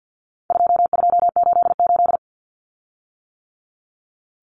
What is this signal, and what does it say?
Morse code, heard as a keyed tone, about 50 wpm, 700 Hz. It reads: 2288
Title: Common Cut Numbers - 4-Digit 50wpm